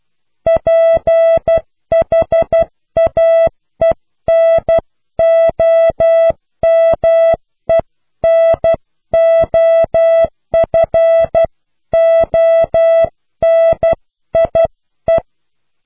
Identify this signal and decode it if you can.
Morse code, heard as a keyed tone, about 12 wpm, 660 Hz. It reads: PHAENOMENOFONIE
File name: cw / Pseudo woorden (15)